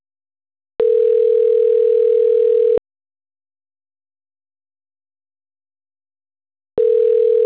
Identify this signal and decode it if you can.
Ringback tone (North America), 440 + 480 Hz.